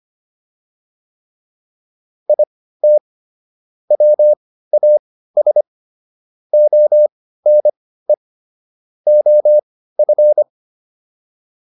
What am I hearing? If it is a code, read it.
Morse code, heard as a keyed tone, about 25 wpm, spaced as for 15 wpm, 600 Hz. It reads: IT WAS ONE OF